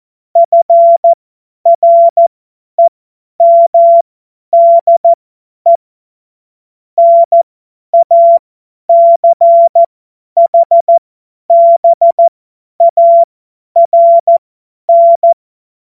Morse code, heard as a keyed tone, about 14 wpm, spaced as for 10 wpm, 680 Hz. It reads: FREMDE NACHBARN
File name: Morsekurs / Witze / Witz 22. Juni 2021, 50 F70